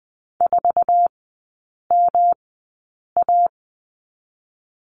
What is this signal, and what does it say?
Morse code, heard as a keyed tone, about 19 wpm, spaced as for 8 wpm, 700 Hz. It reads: 4MA
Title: 127 Lesson 21e - Number 6 - Three Characters 20wpm